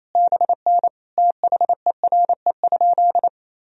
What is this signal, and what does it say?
Morse code, heard as a keyed tone, about 28 wpm, 700 Hz. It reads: BN THERE?
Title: Sentences from Top 100 Words and Abbreviations 28wpm